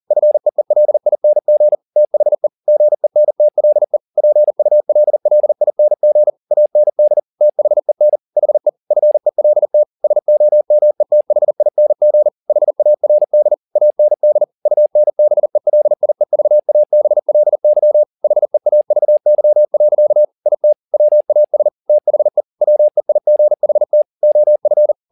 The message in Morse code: FEELING THE GENTLE PULLING AND THEN HE FELT SOMETHING HARD AND UNBELIEVABLY HEAVY. IT WAS THE WEIGHT OF

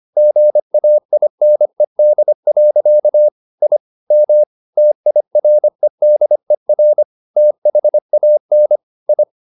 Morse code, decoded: GAINED. I M TIREDER THAN I

